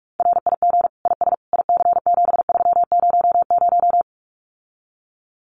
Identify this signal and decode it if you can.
Morse code, heard as a keyed tone, about 46 wpm, 700 Hz. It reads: RIG IS IC7300
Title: Common QSO Phrases Encore 45wpm